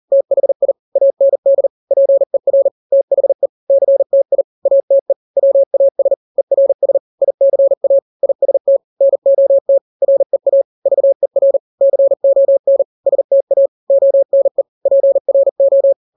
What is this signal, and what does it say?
Morse code, heard as a keyed tone, about 40 wpm, 550 Hz. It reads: THI AND PER THE CTI ATE WAS ERS ICA IST NOT REA VER CON STA ONE PRO
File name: n-grams / 3-grams 40 35 WPM